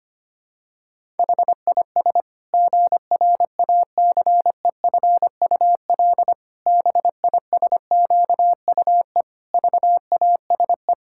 Morse code, heard as a keyed tone, about 25 wpm, 700 Hz. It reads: HIS GRACEFUL BISQUE VASE
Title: Pangram Challenge v2 - 25wpm